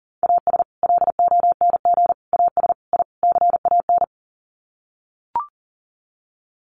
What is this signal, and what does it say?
Morse code, heard as a keyed tone, about 40 wpm, 700 Hz. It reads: AS LONG AS I CAN
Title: Sets of 5 Words Encore2 40wpm